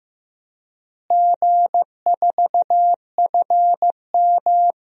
Morse code, heard as a keyed tone, about 15 wpm, 700 Hz. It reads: G4FM
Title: DXCC-100 Call Signs - 4 Characters - Repeated 2x - 15wpm